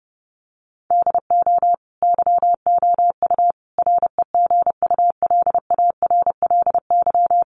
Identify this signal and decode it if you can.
Morse code, heard as a keyed tone, about 30 wpm, 700 Hz. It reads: DO YOU REGULARLY
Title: Sentences from Top 1400 Words 30wpm